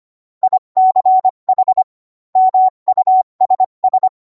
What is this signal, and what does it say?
Morse code, heard as a keyed tone, about 25 wpm, 750 Hz. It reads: ICH MUSS